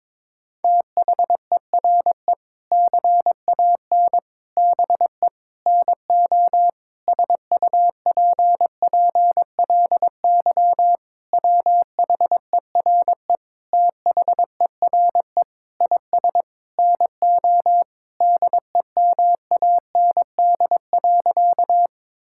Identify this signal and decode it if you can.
Morse code, heard as a keyed tone, about 22 wpm, 700 Hz. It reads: THERE CAN BE NO SUPPLY WHERE THERE IS NO DEMAND.
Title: Sentences from Top 1000 Words - Review Part 5 - 22wpm